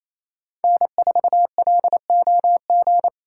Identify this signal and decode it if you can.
Morse code, heard as a keyed tone, about 34 wpm, 700 Hz. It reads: N4LOG
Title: Speed Racing - USA 1x3 Call Signs 28wpm